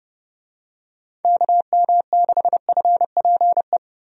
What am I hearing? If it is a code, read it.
Morse code, heard as a keyed tone, about 30 wpm, 700 Hz. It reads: KM6FPE
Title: USA Call Signs - 6 Characters - Repeated 2x - 30wpm